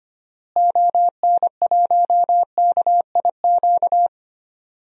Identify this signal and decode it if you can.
Morse code, heard as a keyed tone, about 25 wpm, 700 Hz. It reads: ON1KIQ
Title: DXCC-100 Call Signs - 6 Characters - Repeated 2x - 25wpm